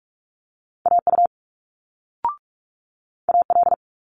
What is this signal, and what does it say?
Morse code, heard as a keyed tone, about 46 wpm, 700 Hz. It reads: AU AL
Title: Multi-letter Phonemes Quick Fire 45wpm